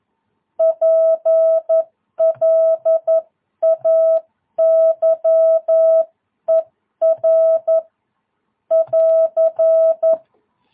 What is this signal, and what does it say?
Morse code, heard as a keyed tone, about 11 wpm, 650 Hz. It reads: PLAYER <AR>
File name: jh8cbh / 2022cw